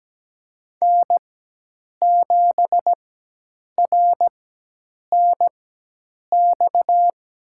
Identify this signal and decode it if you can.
Morse code, heard as a keyed tone, about 17 wpm, spaced as for 8 wpm, 700 Hz. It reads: N7RNX
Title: USA Call Signs - 5 Characters - 4x Character Spacing - 17wpm